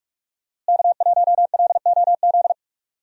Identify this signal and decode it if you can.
Morse code, heard as a keyed tone, about 45 wpm, 700 Hz. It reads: K1LOZ
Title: USA Call Signs - 5 Characters - Repeated 3x - 45wpm